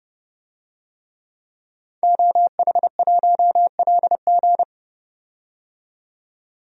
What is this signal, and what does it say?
Morse code, heard as a keyed tone, about 31 wpm, 700 Hz. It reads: OH1LG
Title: DXCC-100 Call Signs - 5 Characters - 30wpm